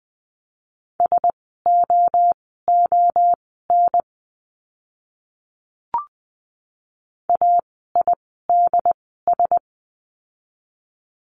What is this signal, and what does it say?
Morse code, heard as a keyed tone, about 20 wpm, spaced as for 15 wpm, 700 Hz. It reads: SOON AIDS
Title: 068 Lesson 12f - Letter d - Words 20wpm